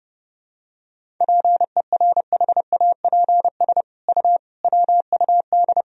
Morse code, heard as a keyed tone, about 31 wpm, 700 Hz. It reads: PERHAPS U WUD
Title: Sentences from Top 500 Words and Abbreviations 30wpm